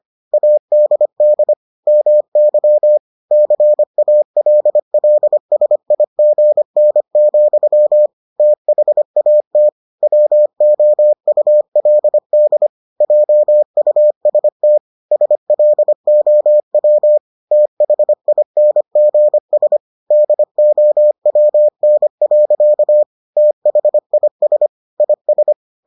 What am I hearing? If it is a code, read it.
Morse code, heard as a keyed tone, about 25 wpm, 600 Hz. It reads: ADD MY CALLSIGN, THAT WOULD JUST SLOW THINGS DOWN. THIS IS